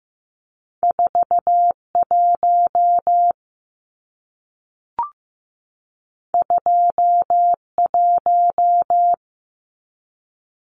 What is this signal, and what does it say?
Morse code, heard as a keyed tone, about 15 wpm, 700 Hz. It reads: 41 21
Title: Numbers - 2-Digit 15wpm